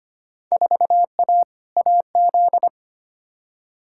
Morse code, heard as a keyed tone, about 25 wpm, 700 Hz. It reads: 4A AZ
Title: Field Day Exchanges - 25wpm